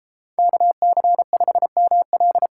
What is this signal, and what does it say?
Morse code, heard as a keyed tone, about 30 wpm, 690 Hz. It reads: KC5ML